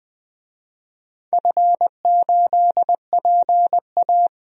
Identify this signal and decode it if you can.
Morse code, heard as a keyed tone, about 20 wpm, 700 Hz. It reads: F8PA
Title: DXCC-100 Call Signs - 4 Characters - 20wpm